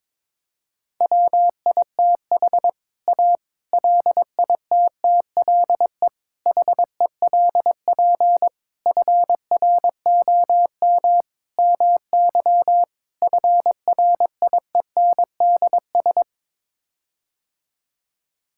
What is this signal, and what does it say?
Morse code, heard as a keyed tone, about 22 wpm, 700 Hz. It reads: WITH A LITTLE HELP FROM MY FRIENDS
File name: practice / 22wpm / Sets of 7 Words - 22wpm